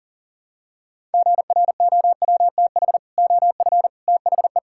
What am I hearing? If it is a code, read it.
Morse code, heard as a keyed tone, about 37 wpm, 700 Hz. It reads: GROWTH OF THE
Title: Sentences from Top 1000 Words - Review Part 5 - 40wpm